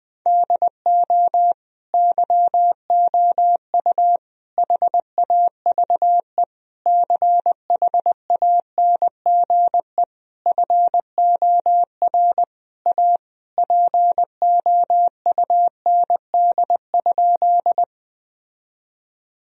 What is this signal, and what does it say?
Morse code, heard as a keyed tone, about 20 wpm, 700 Hz. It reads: DO YOU HAVE CHANGE FOR A POUND?
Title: Sentences from Top 1000 Words - Review Part 2 - 20wpm